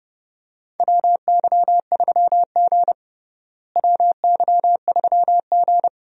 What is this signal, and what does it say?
Morse code, heard as a keyed tone, about 30 wpm, 700 Hz. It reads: WY3G WY3G
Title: USA Call Signs - 4 Characters - Repeated 3x - 30wpm